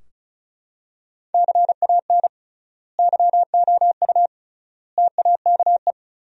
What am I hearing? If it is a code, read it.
Morse code, heard as a keyed tone, about 34 wpm, 700 Hz. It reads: CAN YOU TAKE
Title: 3x Word-Spacing - Sentences from Top 200 Words - 35wpm